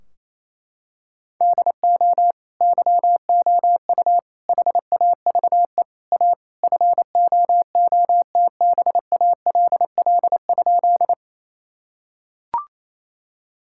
Morse code, heard as a keyed tone, about 28 wpm, 700 Hz. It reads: DO YOU HAVE A FOOTBALL?
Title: Sentences from Top 1600 Words 28wpm